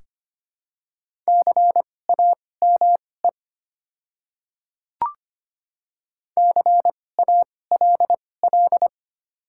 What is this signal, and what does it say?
Morse code, heard as a keyed tone, about 25 wpm, spaced as for 18 wpm, 700 Hz. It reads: CAME CALL
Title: 2x Character Spacing - Top 300 Words - 25wpm